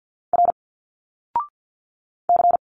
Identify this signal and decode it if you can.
Morse code, heard as a keyed tone, about 50 wpm, 700 Hz. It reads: R C
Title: Single Letters 50wpm